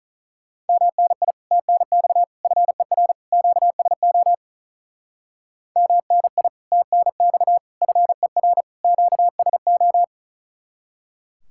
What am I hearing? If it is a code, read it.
Morse code, heard as a keyed tone, about 43 wpm, 700 Hz. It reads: MNI TNX FER QSO MNI TNX FER QSO
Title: Speed-Racing Common QSO Phrases - 35wpm